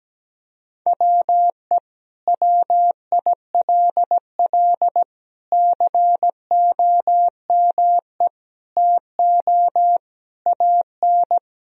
Morse code, heard as a keyed tone, about 17 wpm, 700 Hz. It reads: WE WILL COME TO AN
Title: Sentences from Top 1000 Words - Review Part 3 - 17wpm